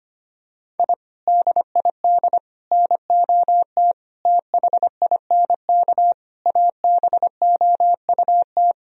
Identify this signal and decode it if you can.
Morse code, heard as a keyed tone, about 25 wpm, 700 Hz. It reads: I DID NOT THINK ABOUT
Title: Sentences from Top 1000 Words - Review Part 4 - 25wpm